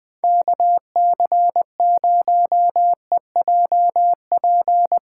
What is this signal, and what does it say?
Morse code, heard as a keyed tone, about 20 wpm, 700 Hz. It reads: KC0EJP
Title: USA Callsign DE Callsign - 6 Characters - 20wpm